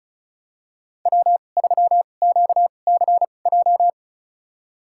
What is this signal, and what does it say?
Morse code, heard as a keyed tone, about 36 wpm, spaced as for 25 wpm, 700 Hz. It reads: W3QCJ
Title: USA Call Signs - 2x Character Spacing - 35wpm